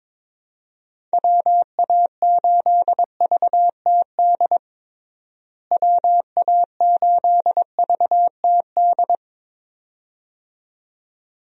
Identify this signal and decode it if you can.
Morse code, heard as a keyed tone, about 21 wpm, 700 Hz. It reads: WA8VTD WA8VTD
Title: USA Call Signs - 6 Characters - Repeated 2x - 22wpm